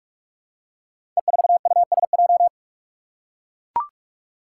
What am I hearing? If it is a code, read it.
Morse code, heard as a keyed tone, about 44 wpm, 700 Hz. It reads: E4USJ
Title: Five Challenging Letter-Number Combinations 45wpm